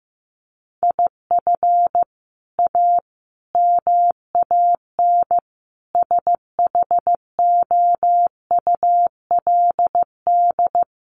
Morse code, heard as a keyed tone, about 14 wpm, 700 Hz. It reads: IF A MAN SHOULD
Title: Sentences from Top 1500 Words 15wpm